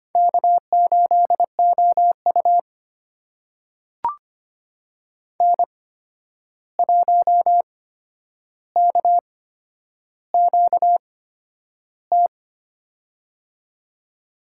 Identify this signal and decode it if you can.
Morse code, heard as a keyed tone, about 25 wpm, spaced as for 10 wpm, 700 Hz. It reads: K8OU N1KQT
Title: USA Call Signs - 8x Character Spacing - 25wpm